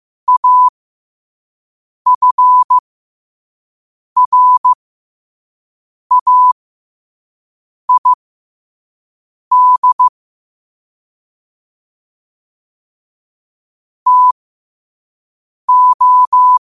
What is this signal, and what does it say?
Morse code, heard as a keyed tone, about 15 wpm, spaced as for 5 wpm, 1000 Hz. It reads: AFRAID TO
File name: qotdfiles / 5wpm